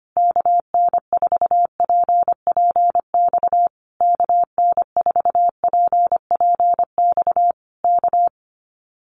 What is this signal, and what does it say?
Morse code, heard as a keyed tone, about 25 wpm, 700 Hz. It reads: KN4PPX KN4PPX K
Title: Common QSO Phrases Encore 25wpm